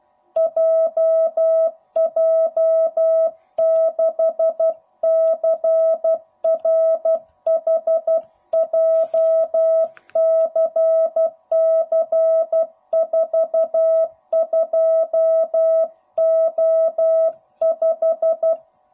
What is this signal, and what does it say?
Morse code, heard as a keyed tone, about 12 wpm, 660 Hz. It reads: JJ6CRH JCC42O5
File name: jh8cbh / 2021cw / cw05b